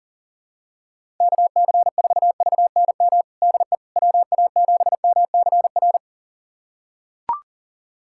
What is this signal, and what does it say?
Morse code, heard as a keyed tone, about 40 wpm, 700 Hz. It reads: KC4VNM DE WA7MCR
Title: USA Callsign DE Callsign - 6 Characters - 40wpm